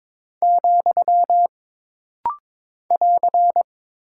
Morse code, heard as a keyed tone, about 23 wpm, 700 Hz. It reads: , <AR>
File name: practice / 22wpm / Single Punctuation-Prosign - Rapid-Fire - 22wpm